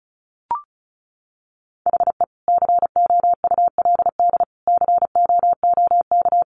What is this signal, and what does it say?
Morse code, heard as a keyed tone, about 34 wpm, 700 Hz. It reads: HE COULD COOK
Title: Sentences from Top 700 Words 35wpm